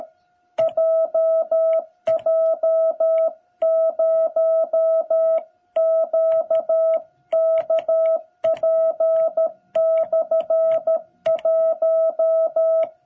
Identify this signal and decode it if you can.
Morse code, heard as a keyed tone, about 13 wpm, 660 Hz. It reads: JJ0QKP/1
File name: jh8cbh / 2021cw